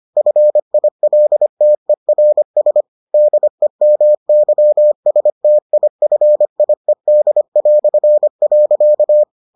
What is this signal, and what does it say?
Morse code, heard as a keyed tone, about 25 wpm, 600 Hz. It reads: FILTERS DEMYSTIFIED".